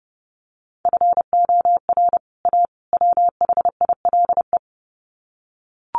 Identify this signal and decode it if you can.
Morse code, heard as a keyed tone, about 30 wpm, 700 Hz. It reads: FOR A WHILE
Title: Sets of 3 Words 30wpm